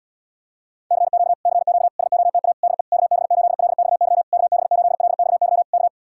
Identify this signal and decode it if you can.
Morse code, heard as a keyed tone, about 80 wpm, 700 Hz. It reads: CQ CQ SOTA DE KB8LXY KB8LXY K